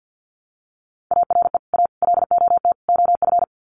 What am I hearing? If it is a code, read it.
Morse code, heard as a keyed tone, about 49 wpm, 700 Hz. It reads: ARE A LOT OF
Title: Sets of 4 Words 50wpm